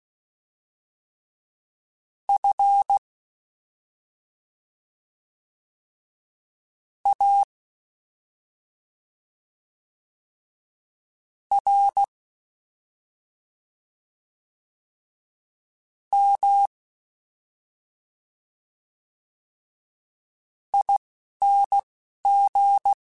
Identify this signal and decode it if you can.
Morse code, heard as a keyed tone, about 16 wpm, spaced as for 3 wpm, 800 Hz. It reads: FARMING
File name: Special INGs-mp3s / Special INGs-mp3s / Slow ing @10